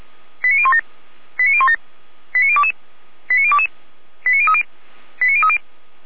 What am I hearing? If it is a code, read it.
Five-tone selective call (ZVEI), heard as quick runs of five tones, tones of about 70 ms. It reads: 89017 89017 89022 89022 89030 89030